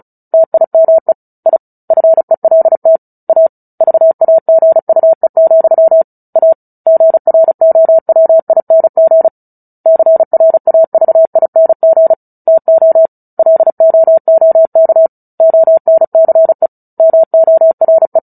Morse code, heard as a keyed tone, about 34 wpm, 650 Hz. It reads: TIME I FELT A VAGUE, A GROWING CRAVING TO LOOK ONCE MORE
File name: Book Two / 09-Chapter 9 - 35 WPM